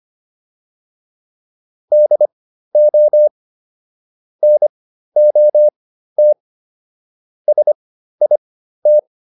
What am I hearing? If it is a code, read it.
Morse code, heard as a keyed tone, about 25 wpm, spaced as for 13 wpm, 600 Hz. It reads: DO NOT SIT